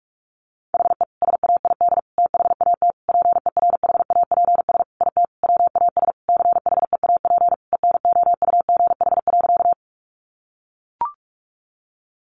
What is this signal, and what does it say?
Morse code, heard as a keyed tone, about 44 wpm, 700 Hz. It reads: HE SAID THAT PERHAPS IT WAS CHEAP ENOUGH.
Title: Sentences from Top 1200 Words 45wpm